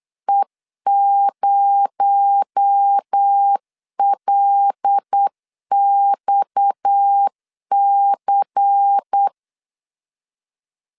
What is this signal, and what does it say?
Morse code, heard as a keyed tone, about 8 wpm, 800 Hz. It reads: E0LXC
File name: LEZIONI / 10 SIGNS / 10B INT 8wpm 3534